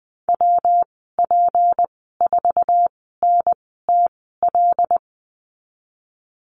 Morse code, heard as a keyed tone, about 19 wpm, spaced as for 14 wpm, 700 Hz. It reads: WP4NTL
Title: 179 Lesson 29g - Slash - Call Signs 20wpm